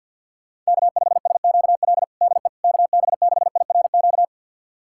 Morse code, heard as a keyed tone, about 52 wpm, 700 Hz. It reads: K5SQL DE KB6IRQ
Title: USA Callsign DE Callsign 50wpm